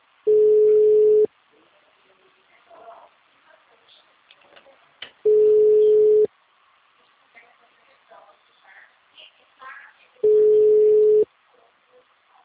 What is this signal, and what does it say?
Ringback tone (Europe, 425 Hz), with speech.
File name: out- -2004- - - .25